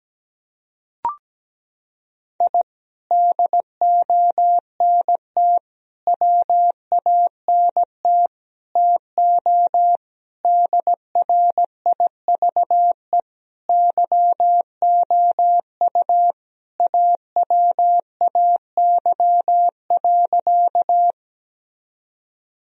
Morse code, heard as a keyed tone, about 17 wpm, 700 Hz. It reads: I DONT WANT TO DRIVE YOU AWAY.
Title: Sentences from Top 1000 Words - Review Part 2 - 17wpm